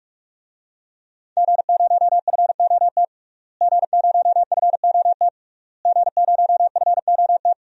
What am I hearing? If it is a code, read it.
Morse code, heard as a keyed tone, about 41 wpm, 700 Hz. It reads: G0FOT G0FOT G0FOT
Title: DXCC-100 Call Signs - 5 Characters - Repeated 3x - 45wpm